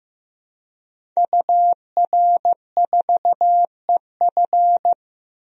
Morse code, heard as a keyed tone, about 15 wpm, 700 Hz. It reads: UR4EF
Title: DXCC-100 Call Signs - 5 Characters - Repeated 2x - 15wpm